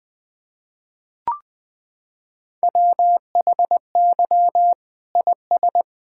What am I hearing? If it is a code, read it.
Morse code, heard as a keyed tone, about 20 wpm, 700 Hz. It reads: WHY IS
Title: Sentences from Top 1600 Words 20wpm